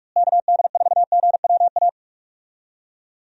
Morse code, heard as a keyed tone, about 43 wpm, 700 Hz. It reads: KD4GWA
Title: USA Call Signs - 6 Characters - Repeated 2x - 45wpm